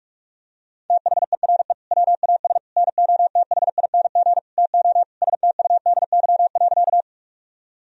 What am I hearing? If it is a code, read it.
Morse code, heard as a keyed tone, about 46 wpm, 700 Hz. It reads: THERE WAS NOTHING TO STUDY.
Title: Sentences from Top 1000 Words - Review Part 1 - 45wpm